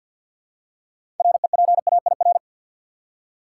Morse code, heard as a keyed tone, about 52 wpm, 700 Hz. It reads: REPAIR
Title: Six Letter Words - 50wpm